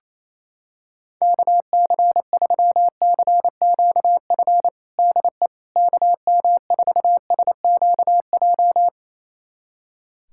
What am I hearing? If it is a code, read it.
Morse code, heard as a keyed tone, about 28 wpm, 700 Hz. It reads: KC3CQF DE KM4SQJ
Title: USA Callsign DE Callsign - 6 Characters - 28wpm